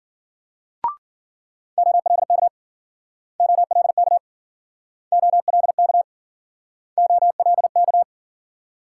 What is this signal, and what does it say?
Morse code, heard as a keyed tone, about 53 wpm, 700 Hz. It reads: OLK OLK OLK OLK